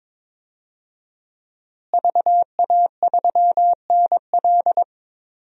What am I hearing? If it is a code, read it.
Morse code, heard as a keyed tone, about 22 wpm, 700 Hz. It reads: VA3NL
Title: DXCC-100 Call Signs - 5 Characters - Repeated 3x - 22wpm